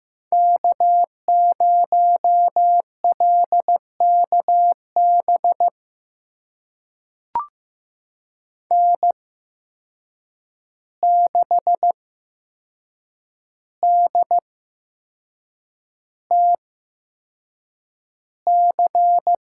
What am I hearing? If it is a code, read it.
Morse code, heard as a keyed tone, about 15 wpm, spaced as for 6 wpm, 700 Hz. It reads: K0LKB N6DTC
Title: USA Call Signs - 5 Characters - 8x Character Spacing - 15wpm